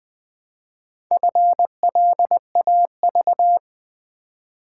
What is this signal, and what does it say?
Morse code, heard as a keyed tone, about 20 wpm, 700 Hz. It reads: FLAV
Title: Build A Word - 2 Letters At A Time - 6-Letter Words - 20wpm